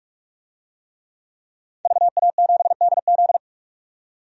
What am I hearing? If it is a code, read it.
Morse code, heard as a keyed tone, about 46 wpm, 700 Hz. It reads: VA7DZ
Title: DXCC-100 Call Signs - 5 Characters - 45wpm